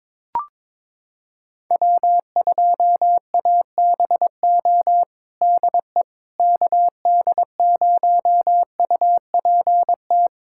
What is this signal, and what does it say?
Morse code, heard as a keyed tone, about 22 wpm, 700 Hz. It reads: W2ABO DE KD0UPT
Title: USA Callsign DE Callsign 22wpm